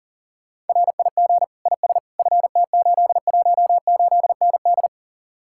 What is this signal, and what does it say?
Morse code, heard as a keyed tone, about 50 wpm, 700 Hz. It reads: RIG IS FT818ND
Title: Speed-Racing Common QSO Phrases - 40wpm